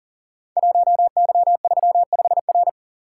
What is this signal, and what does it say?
Morse code, heard as a keyed tone, about 39 wpm, 700 Hz. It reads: 1Y35R
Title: Code Groups - 5 Letter:Numbers - 40wpm